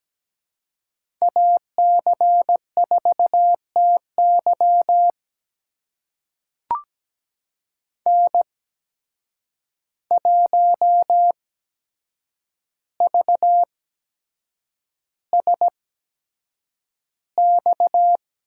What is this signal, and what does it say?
Morse code, heard as a keyed tone, about 17 wpm, spaced as for 7 wpm, 700 Hz. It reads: AC4TY N1VSX
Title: USA Call Signs - 5 Characters - 8x Character Spacing - 17wpm